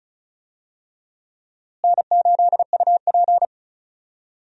Morse code, heard as a keyed tone, about 36 wpm, 700 Hz. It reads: N8UP
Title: USA Call Signs - 4 Characters - Repeated 3x - 35wpm